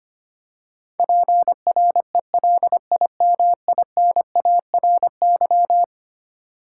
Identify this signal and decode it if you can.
Morse code, heard as a keyed tone, about 25 wpm, 700 Hz. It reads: PRELIMINARY
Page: morse-code-ninja-practice-25wpm